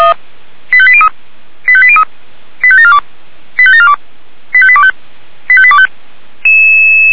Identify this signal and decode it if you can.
Five-tone selective call (ZVEI), heard as quick runs of five tones, tones of about 70 ms. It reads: 85803 85803 85742 85742 85825 85825
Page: ZVEI